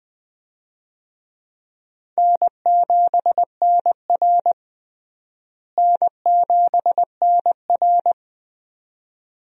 Morse code, heard as a keyed tone, about 21 wpm, 700 Hz. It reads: N7NR N7NR
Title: USA Call Signs - 4 Characters - Repeated 2x - 20wpm